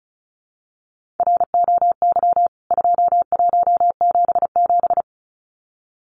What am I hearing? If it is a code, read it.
Morse code, heard as a keyed tone, about 34 wpm, 700 Hz. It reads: ROY 2177
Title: CWT Exchange 35wpm